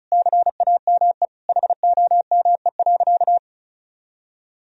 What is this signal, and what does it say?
Morse code, heard as a keyed tone, about 36 wpm, 700 Hz. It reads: CAME HOME.
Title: Sentences from Top 1000 Words - Review Part 4 - 35wpm